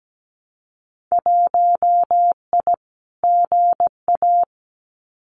Morse code, heard as a keyed tone, about 17 wpm, 700 Hz. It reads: 1I GA
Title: Winter Field Day Exchanges Encore 17wpm